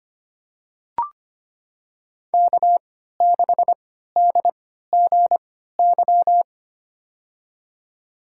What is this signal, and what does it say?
Morse code, heard as a keyed tone, about 25 wpm, spaced as for 14 wpm, 700 Hz. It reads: K6DGY
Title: USA Call Signs - 5 Characters - 3x Character Spacing - 25wpm